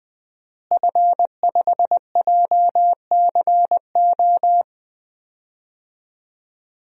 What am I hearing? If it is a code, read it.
Morse code, heard as a keyed tone, about 21 wpm, 700 Hz. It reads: F5JCO
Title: DXCC-100 Call Signs - 5 Characters - Repeated 3x - 20wpm